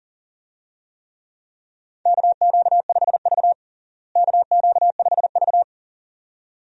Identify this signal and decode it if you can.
Morse code, heard as a keyed tone, about 40 wpm, 700 Hz. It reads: KQ5V KQ5V
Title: USA Call Signs - 4 Characters - Repeated 2x - 40wpm